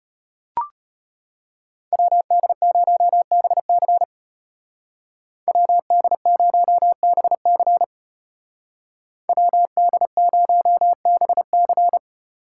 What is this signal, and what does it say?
Morse code, heard as a keyed tone, about 32 wpm, 700 Hz. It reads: WD0BC WD0BC WD0BC